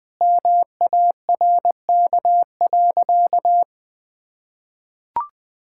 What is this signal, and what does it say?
Morse code, heard as a keyed tone, about 20 wpm, 700 Hz. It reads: MARK.
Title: Sentences from Top 1000 Words - Review Part 6 - 20wpm